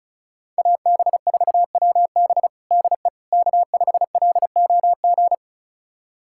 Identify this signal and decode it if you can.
Morse code, heard as a keyed tone, about 34 wpm, 700 Hz. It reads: AB4WB DE K5LOG
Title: USA Callsign DE Callsign - 5 Characters - 35wpm